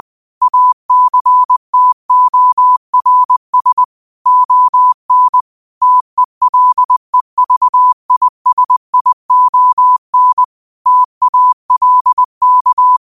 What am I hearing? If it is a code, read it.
Morse code, heard as a keyed tone, about 20 wpm, 1000 Hz. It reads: ACTORS ON TELEVISION TALK
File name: qotdfiles / 20wpm